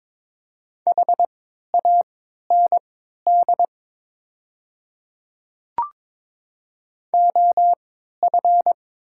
Morse code, heard as a keyed tone, about 22 wpm, spaced as for 12 wpm, 700 Hz. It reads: HAND OF
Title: 3x Character Spacing - Top 200 Words - 22wpm